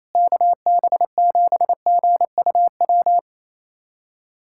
Morse code, heard as a keyed tone, about 28 wpm, 700 Hz. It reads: KB7GUW
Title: USA Callsign DE Callsign - 6 Characters - 28wpm